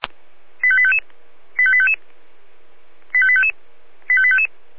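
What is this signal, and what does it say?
Five-tone selective call (ZVEI), heard as quick runs of five tones, tones of about 70 ms. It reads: 85855 85855 85855 85855
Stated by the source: ZVEI five-tone selective call 85320, received on SDR.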